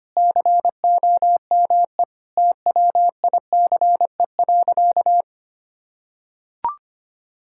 Morse code, heard as a keyed tone, about 25 wpm, 700 Hz. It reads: COME TWICE.
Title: Sentences from Top 1600 Words 25wpm